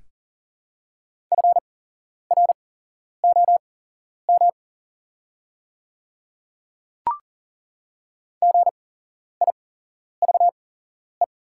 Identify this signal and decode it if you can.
Morse code, heard as a keyed tone, about 40 wpm, spaced as for 11 wpm, 690 Hz. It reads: FROM GIVE
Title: 8x Character Spacing - Top 200 Words - 40wpm